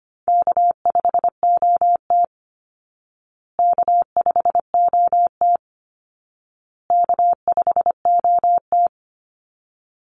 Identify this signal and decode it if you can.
Morse code, heard as a keyed tone, about 25 wpm, 700 Hz. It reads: K5OT K5OT K5OT
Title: Easy Call Signs Repeated 3x - 25wpm